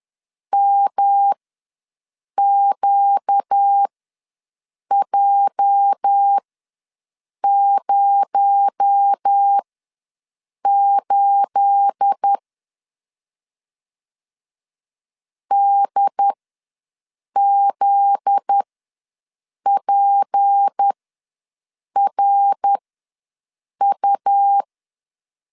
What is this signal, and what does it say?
Morse code, heard as a keyed tone, about 10 wpm, spaced as for 6 wpm, 800 Hz. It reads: MQJ08 DZPRU
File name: LEZIONI / 9 2867 ws 10wpm 4215